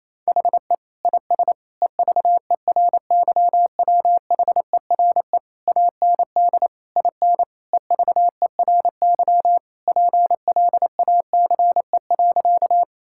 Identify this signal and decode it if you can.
Morse code, heard as a keyed tone, about 28 wpm, 700 Hz. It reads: HE IS EVERYWHERE AND IN EVERY PLACE.
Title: Sentences from Top 1600 Words 28wpm